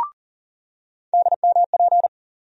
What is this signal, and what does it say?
Morse code, heard as a keyed tone, about 40 wpm, 700 Hz. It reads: DMP
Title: USA Call Sign Postfixes - 3 Letters 40wpm